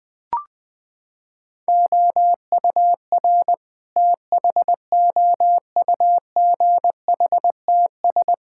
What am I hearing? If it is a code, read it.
Morse code, heard as a keyed tone, about 21 wpm, 700 Hz. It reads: OUR THOUGHTS
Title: Sentences with 7-Word Phrases - 20wpm